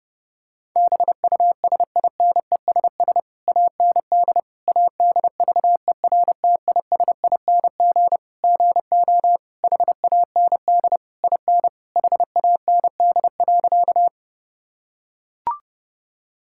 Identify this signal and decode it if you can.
Morse code, heard as a keyed tone, about 30 wpm, 700 Hz. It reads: BUSINESS AND ADVERTISING GO HAND IN HAND.
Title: Sentences from Top 1000 Words - Review Part 6 - 30wpm